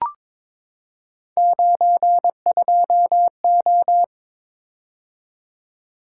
Morse code, heard as a keyed tone, about 22 wpm, 700 Hz. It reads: 92O
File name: practice / 22wpm / Three Letter-Number Combinations - 22wpm